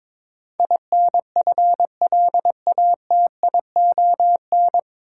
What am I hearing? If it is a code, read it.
Morse code, heard as a keyed tone, about 22 wpm, 700 Hz. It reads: INFLATION
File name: 9 Letter Words - First 3 Letters Given - 22wpm